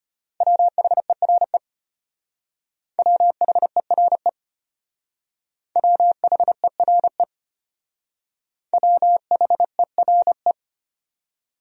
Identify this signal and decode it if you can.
Morse code, heard as a keyed tone, about 30 wpm, 700 Hz. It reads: WHERE WHERE WHERE WHERE
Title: Speed-Racing Top 300 Words 25wpm